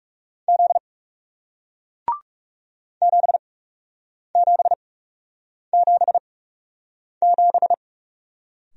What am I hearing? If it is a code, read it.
Morse code, heard as a keyed tone, about 38 wpm, spaced as for 8 wpm, 700 Hz. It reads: Z 7777